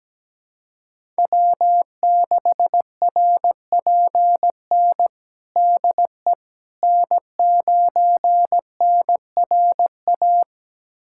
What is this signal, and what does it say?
Morse code, heard as a keyed tone, about 17 wpm, 700 Hz. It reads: W6RPN DE N9NRA
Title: USA Callsign DE Callsign - 5 Characters - 17wpm